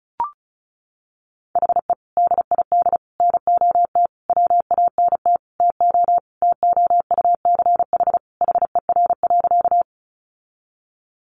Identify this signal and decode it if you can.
Morse code, heard as a keyed tone, about 34 wpm, 700 Hz. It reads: HE DID NOT WANT TO TOUCH HER.
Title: Sentences from Top 1200 Words 35wpm